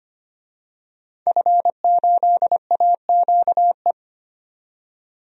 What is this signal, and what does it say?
Morse code, heard as a keyed tone, about 25 wpm, 700 Hz. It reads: F8AQE
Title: DXCC-100 Call Signs - 5 Characters - 25wpm